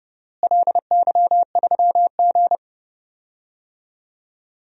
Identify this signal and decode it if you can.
Morse code, heard as a keyed tone, about 30 wpm, 700 Hz. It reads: LY3G
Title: DXCC-100 Call Signs - 4 Characters - Repeated 2x - 30wpm